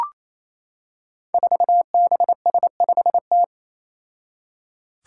Morse code, heard as a keyed tone, about 28 wpm, 700 Hz. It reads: 4BS5T
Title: Five Challenging Letter-Number Combinations 28wpm